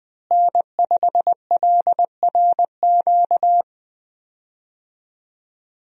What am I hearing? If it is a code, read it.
Morse code, heard as a keyed tone, about 21 wpm, 700 Hz. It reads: N5LRQ
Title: USA Call Signs - 5 Characters - Repeated 2x - 20wpm